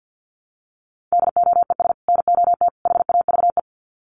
Morse code, heard as a keyed tone, about 50 wpm, 700 Hz. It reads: DOES NOT HAVE
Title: Sets of 3 Words 50wpm